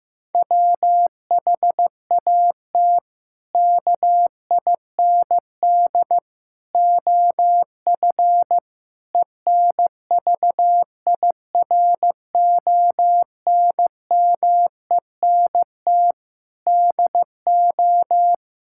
Morse code, heard as a keyed tone, about 15 wpm, 700 Hz. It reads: WHAT KIND OF ENVIRONMENT DO
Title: Sentences from Top 1000 Words - Review Part 5 - 15wpm